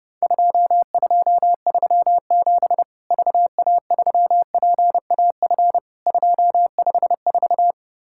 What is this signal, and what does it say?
Morse code, heard as a keyed tone, about 30 wpm, 700 Hz. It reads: 2237 VA3PAF 254
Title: Worked All Europe DX Contest - QTC Exchange - 30wpm